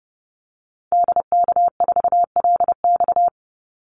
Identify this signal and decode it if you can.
Morse code, heard as a keyed tone, about 30 wpm, 700 Hz. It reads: DK4LX
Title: Call Signs - 30wpm